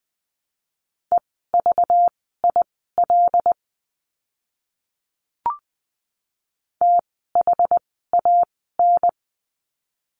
Morse code, heard as a keyed tone, about 20 wpm, spaced as for 14 wpm, 700 Hz. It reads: EVIL THAN
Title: 192 Lesson 31f - Letter v - Words 20wpm